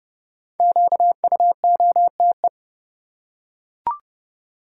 Morse code, heard as a keyed tone, about 30 wpm, 700 Hz. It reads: QUOTE
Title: Five Letter Words - 30wpm